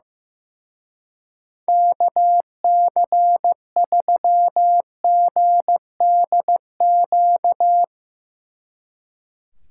Morse code, heard as a keyed tone, about 15 wpm, 700 Hz. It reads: KC3GDQ
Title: Speed Racing - USA 2x3 Call Signs 15wpm